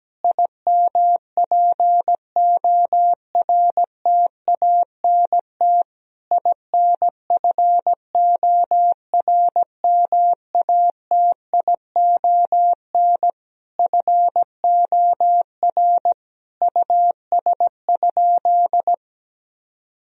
Morse code, heard as a keyed tone, about 17 wpm, 700 Hz. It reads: IMPORTANT INFORMATION FOR US?
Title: Sentences from Top 1000 Words - Review Part 3 - 17wpm